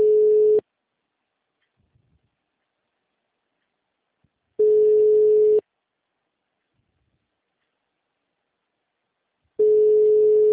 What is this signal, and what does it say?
Ringback tone (Europe, 425 Hz).